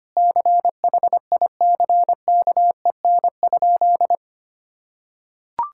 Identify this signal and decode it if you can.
Morse code, heard as a keyed tone, about 25 wpm, 700 Hz. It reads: CHICKEN?
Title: Sentences from Top 1000 Words - Review Part 5 - 25wpm